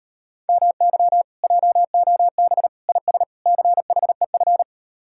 Morse code, heard as a keyed tone, about 38 wpm, 700 Hz. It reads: MY JOB IS CHEF